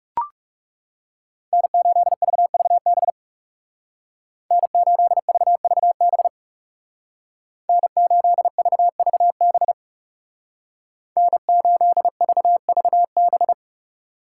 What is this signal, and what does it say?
Morse code, heard as a keyed tone, about 36 wpm, 700 Hz. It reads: N8VVB N8VVB N8VVB N8VVB
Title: Speed Racing - USA Call Signs 30wpm